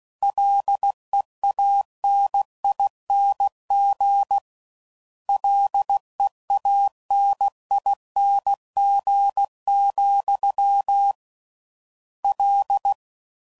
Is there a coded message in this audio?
Yes: Morse code from 0.2 to 12.9 s, heard as a keyed tone, about 16 wpm, 800 Hz. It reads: LEANING LEANING, L